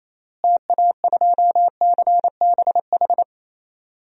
Morse code, heard as a keyed tone, about 27 wpm, 700 Hz. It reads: TA2CBH
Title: DXCC-100 Call Signs - 6 Characters - Repeated 2x - 28wpm